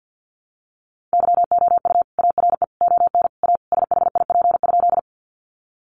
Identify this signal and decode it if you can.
Morse code, heard as a keyed tone, about 50 wpm, 700 Hz. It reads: YOU ARE ON A SHIP?
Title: Sentences from Top 400 Words 50wpm